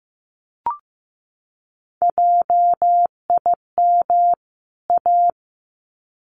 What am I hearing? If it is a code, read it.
Morse code, heard as a keyed tone, about 15 wpm, 700 Hz. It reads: JIM A
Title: CWT Exchange 15wpm